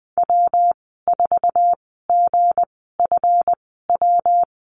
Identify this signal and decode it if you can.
Morse code, heard as a keyed tone, about 20 wpm, spaced as for 14 wpm, 700 Hz. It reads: W4GFW
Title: 161 Lesson 26g - Letter g - Call Signs 20wpm